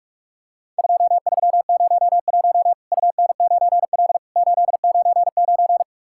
Morse code, heard as a keyed tone, about 44 wpm, 700 Hz. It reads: 2301 UN9L 899
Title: Worked All Europe DX Contest - QTC Exchange - 45wpm